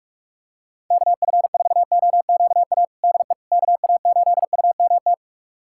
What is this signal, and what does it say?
Morse code, heard as a keyed tone, about 46 wpm, 700 Hz. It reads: KF4OQA DE KA8UMT
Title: USA Callsign DE Callsign - 6 Characters - 45wpm